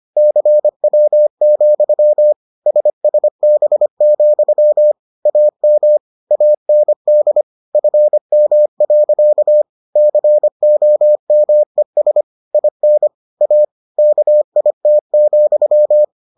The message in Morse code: CW, SSB, AM AND FM. COMES IN A KIT,

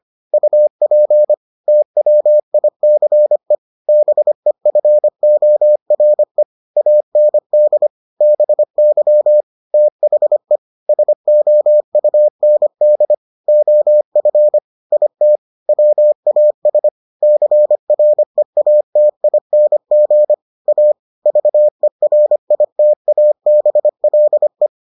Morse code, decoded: UP TWICE BEFORE AND BY THE SOUND OF IT WAS CREATING A VERITABLE

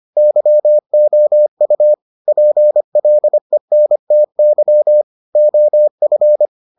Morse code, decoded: YOU PLENTY OF

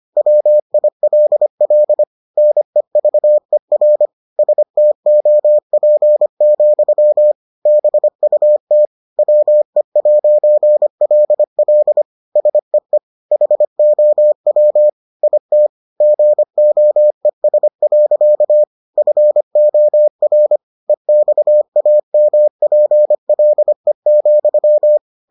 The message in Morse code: WILL NEVER STOP, BUT WE'LL SEE HOW IT GOES. FOR EXAMPLE,

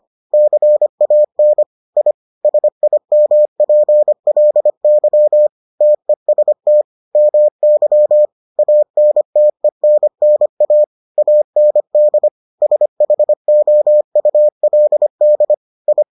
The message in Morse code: CAN I SIMPLY TEST MY ANTENNA AND SHOULD I

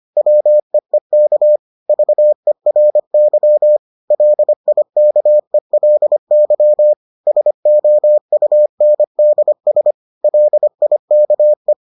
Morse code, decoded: WEEK VERY LIKELY SOUNDS LIKE

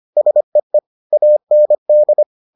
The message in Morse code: SEE AND

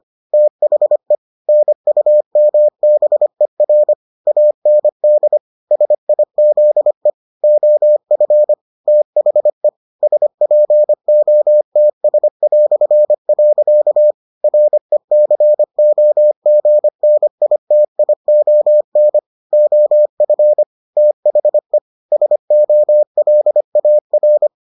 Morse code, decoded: THE NUMBER AND SIZE OF THE SPOTS". RECOGNITION OF THE SOLAR

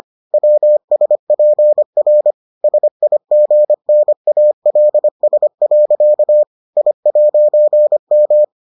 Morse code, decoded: WSPR SIGNALS. I'M

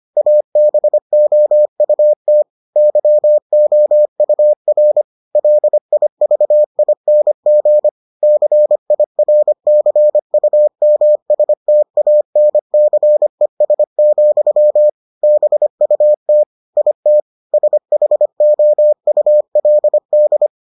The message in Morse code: ABOUT YOUR LIVING CIRCUMSTANCES, BUT IT SHOULD

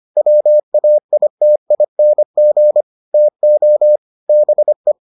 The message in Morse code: WAITING TO BE

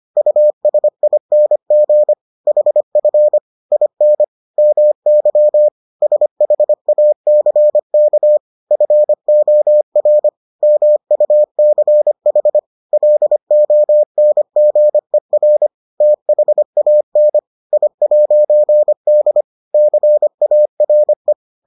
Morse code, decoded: USING HF IN MY SHACK FOR MUCH LONGER THAN I'D CARE

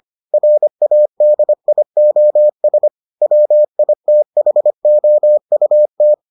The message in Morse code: RADIOS WITHOUT

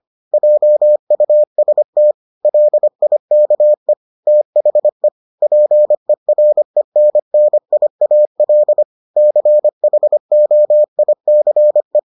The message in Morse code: JUST LIKE THE PERENNIAL CHOICE

